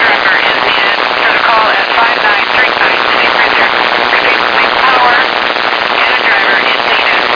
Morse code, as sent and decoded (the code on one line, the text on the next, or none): none